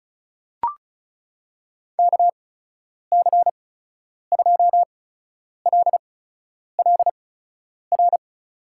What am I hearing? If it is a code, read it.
Morse code, heard as a keyed tone, about 36 wpm, spaced as for 10 wpm, 700 Hz. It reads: KC2LLR
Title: USA Call Signs - 6 Characters - 8x Character Spacing - 35wpm